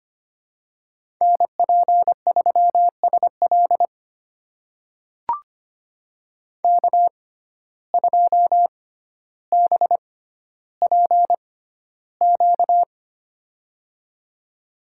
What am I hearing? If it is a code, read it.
Morse code, heard as a keyed tone, about 25 wpm, spaced as for 13 wpm, 700 Hz. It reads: NP3SL K2BPQ
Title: USA Call Signs - 5 Characters - 6x Character Spacing - 25wpm